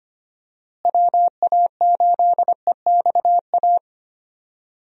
Morse code, heard as a keyed tone, about 25 wpm, 700 Hz. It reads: WA8EXA
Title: USA Call Signs - 6 Characters - Repeated 2x - 25wpm